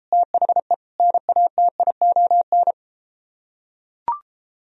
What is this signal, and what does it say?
Morse code, heard as a keyed tone, about 33 wpm, 700 Hz. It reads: THE NATION